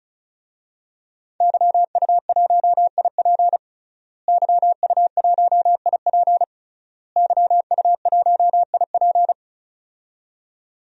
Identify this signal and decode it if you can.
Morse code, heard as a keyed tone, about 36 wpm, 700 Hz. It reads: YU1IP YU1IP YU1IP
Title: DXCC-100 Call Signs - 5 Characters - Repeated 3x - 35wpm